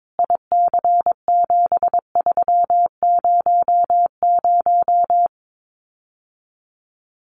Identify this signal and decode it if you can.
Morse code, heard as a keyed tone, about 22 wpm, 700 Hz. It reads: IC7300
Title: Common QSO Phrases Encore 22wpm